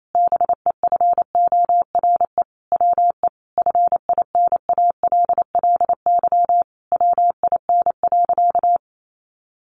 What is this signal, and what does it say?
Morse code, heard as a keyed tone, about 28 wpm, 700 Hz. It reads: BEFORE WE FINALLY WIN.
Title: Sentences from Top 1500 Words 28wpm